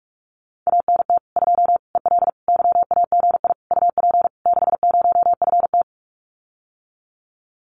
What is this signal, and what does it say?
Morse code, heard as a keyed tone, about 46 wpm, 700 Hz. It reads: ANT 2 EL YAGI UP 60FT
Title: Common QSO Phrases Encore 45wpm